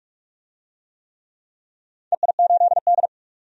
Morse code, heard as a keyed tone, about 41 wpm, 700 Hz. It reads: EI8D
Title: DXCC-100 Call Signs - 4 Characters - 45wpm